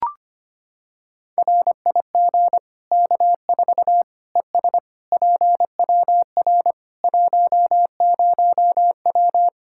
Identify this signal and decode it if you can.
Morse code, heard as a keyed tone, about 25 wpm, 700 Hz. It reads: RIG K4 ES PWR 10W
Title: Practice QSO - Exchange 2 - Info - 25wpm